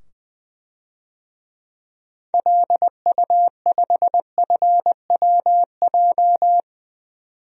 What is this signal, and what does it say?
Morse code, heard as a keyed tone, about 21 wpm, 700 Hz. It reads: LU5FWJ
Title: DXCC-100 Call Signs - 6 Characters - 20wpm